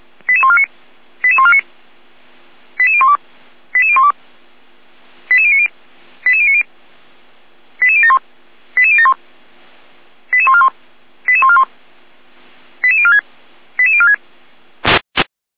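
Five-tone selective call (ZVEI), heard as quick runs of five tones, tones of about 70 ms. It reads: 88149 88149 88012 88012 88090 88090 88071 88071 88141 88141 88046 88046
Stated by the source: ZVEI five-tone selective call received on SDR.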